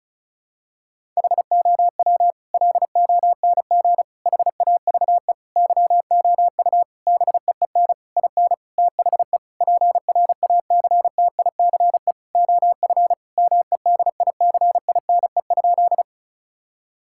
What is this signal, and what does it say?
Morse code, heard as a keyed tone, about 36 wpm, 700 Hz. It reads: HOW LONG HAVE YOU BEEN IN THE PRACTICE OF MEDICINE?
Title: Sentences from Top 1000 Words - Review Part 6 - 35wpm